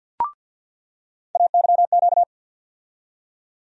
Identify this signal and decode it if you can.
Morse code, heard as a keyed tone, about 47 wpm, 700 Hz. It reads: AYQ
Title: USA Call Sign Postfixes - 3 Letters 50wpm